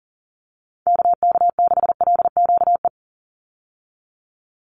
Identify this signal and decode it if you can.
Morse code, heard as a keyed tone, about 37 wpm, 700 Hz. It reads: KK6LQE
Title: USA Call Signs 40wpm